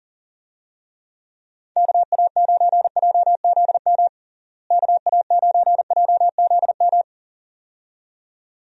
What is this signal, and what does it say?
Morse code, heard as a keyed tone, about 40 wpm, 700 Hz. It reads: KA9JZM KA9JZM
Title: USA Call Signs - 6 Characters - Repeated 2x - 40wpm